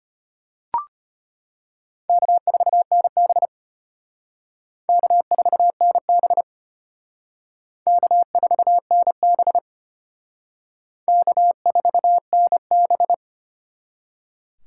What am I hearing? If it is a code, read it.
Morse code, heard as a keyed tone, about 32 wpm, 700 Hz. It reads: K4NB K4NB K4NB K4NB